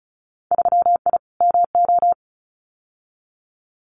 Morse code, heard as a keyed tone, about 34 wpm, 700 Hz. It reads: 3I MO
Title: Winter Field Day Exchanges 35wpm